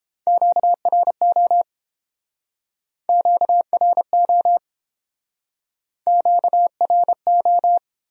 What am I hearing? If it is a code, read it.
Morse code, heard as a keyed tone, about 30 wpm, 700 Hz. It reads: QRO QRO QRO